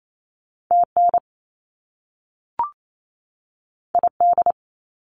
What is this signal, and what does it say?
Morse code, heard as a keyed tone, about 28 wpm, 700 Hz. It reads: TN ID
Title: US States 28wpm